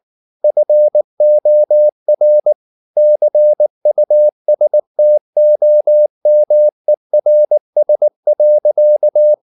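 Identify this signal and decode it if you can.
Morse code, heard as a keyed tone, about 19 wpm, 600 Hz. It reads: FOR CUSTOMERS.
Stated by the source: Morse practice at 18 wpm, English text, from CW med Gnister.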